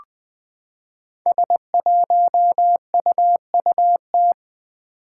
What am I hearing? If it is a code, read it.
Morse code, heard as a keyed tone, about 20 wpm, 700 Hz. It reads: S1UUT
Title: Five Challenging Letter-Number Combinations 20wpm